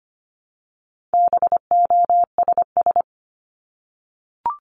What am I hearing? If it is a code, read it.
Morse code, heard as a keyed tone, about 25 wpm, 700 Hz. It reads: BOSS
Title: Top 501-750 Words 25wpm